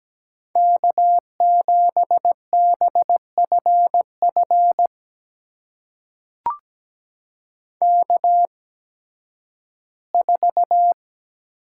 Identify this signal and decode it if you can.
Morse code, heard as a keyed tone, about 17 wpm, spaced as for 11 wpm, 700 Hz. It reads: K7BFF K4
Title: USA Call Signs - 5 Characters - 8x Character Spacing - 17wpm